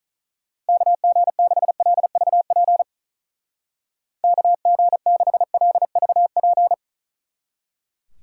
Morse code, heard as a keyed tone, about 44 wpm, 700 Hz. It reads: KG6LVP KG6LVP